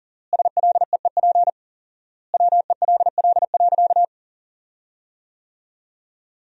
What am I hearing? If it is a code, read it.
Morse code, heard as a keyed tone, about 40 wpm, 700 Hz. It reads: SLEEP WELL.
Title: 4x Word-Spacing - Sentences from Top 500 Words - 40wpm